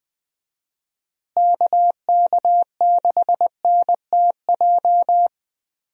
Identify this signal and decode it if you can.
Morse code, heard as a keyed tone, about 20 wpm, 700 Hz. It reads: KK6NTJ
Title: USA Call Signs - 6 Characters - Repeated 3x - 20wpm